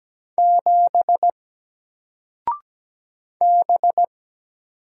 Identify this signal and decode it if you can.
Morse code, heard as a keyed tone, about 17 wpm, 700 Hz. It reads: 7 B
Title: Single Letter-Number Alt 17wpm